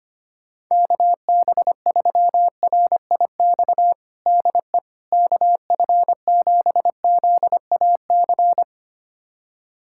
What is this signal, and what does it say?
Morse code, heard as a keyed tone, about 25 wpm, 700 Hz. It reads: KB3RIX DE KF7ZAC
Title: USA Callsign DE Callsign - 6 Characters - 25wpm